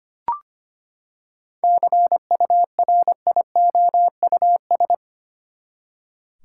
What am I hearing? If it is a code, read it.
Morse code, heard as a keyed tone, about 25 wpm, 700 Hz. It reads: CURIOUS
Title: Seven Letter Words - 25wpm